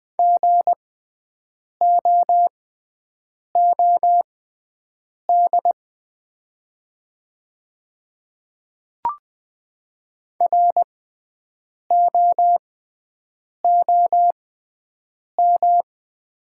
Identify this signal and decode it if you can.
Morse code, heard as a keyed tone, about 20 wpm, spaced as for 7 wpm, 700 Hz. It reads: GOOD ROOM
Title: 6x Character Spacing - Top 300 Words - 20wpm